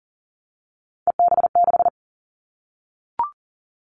Morse code, heard as a keyed tone, about 42 wpm, 700 Hz. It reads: EB6
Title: Callsign Prefixes for Top 100 DXCC - 40wpm